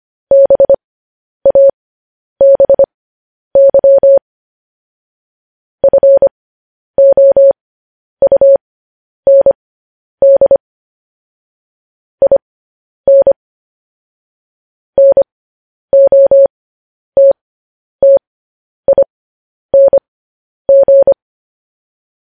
Morse code, heard as a keyed tone, about 25 wpm, spaced as for 10 wpm, 550 Hz. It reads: BABY FOUND IN NOTTING
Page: news-headlines-in-morse-code-at-10-wpm